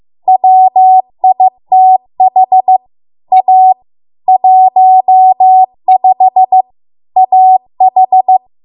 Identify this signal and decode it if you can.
Morse code, heard as a keyed tone, about 15 wpm, 750 Hz. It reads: WITH A 15 AH